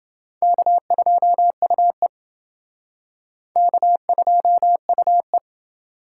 Morse code, heard as a keyed tone, about 28 wpm, 700 Hz. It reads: K2UE K2UE